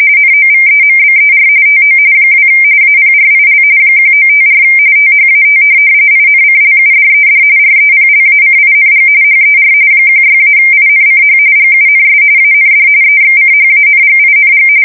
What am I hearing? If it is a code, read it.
RTTY, about 45.45 baud. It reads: NOVEMBER TO 2 APRIL 1954 THAT INCLUDED REPLACING HER 16-INCH GUNS AND EXCHANGING
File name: fsk170 / USS-Missouri